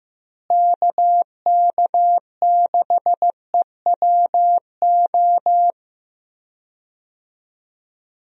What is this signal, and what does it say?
Morse code, heard as a keyed tone, about 15 wpm, 700 Hz. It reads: KK6EWO
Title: USA Call Signs - 6 Characters - Repeated 2x - 15wpm